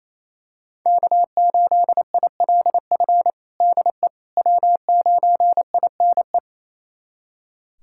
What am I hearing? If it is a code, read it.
Morse code, heard as a keyed tone, about 27 wpm, 700 Hz. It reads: K8ILF DE W9INE
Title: USA Callsign DE Callsign - 5 Characters - 28wpm